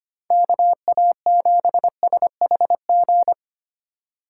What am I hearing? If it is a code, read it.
Morse code, heard as a keyed tone, about 28 wpm, 700 Hz. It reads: KA7SHG
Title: Speed Racing - USA 2x3 Call Signs 25wpm